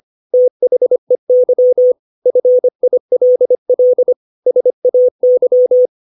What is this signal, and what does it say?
Morse code, heard as a keyed tone, about 25 wpm, 500 Hz. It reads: THEY FILL SAY
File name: 632 words / 632 Words 25WPM